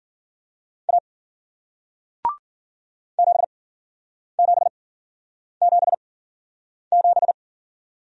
Morse code, heard as a keyed tone, about 53 wpm, spaced as for 9 wpm, 700 Hz. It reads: A 7777